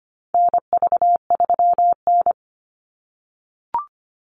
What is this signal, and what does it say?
Morse code, heard as a keyed tone, about 25 wpm, 700 Hz. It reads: NV3N
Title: Call Signs - 25wpm